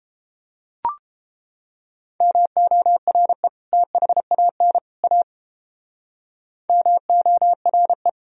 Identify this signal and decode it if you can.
Morse code, heard as a keyed tone, about 33 wpm, 700 Hz. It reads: MORE THAN A MORE